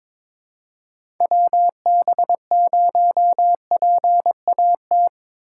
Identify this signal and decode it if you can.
Morse code, heard as a keyed tone, about 23 wpm, 700 Hz. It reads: WB0PAT
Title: USA Call Signs - 6 Characters - 4x Character Spacing - 22wpm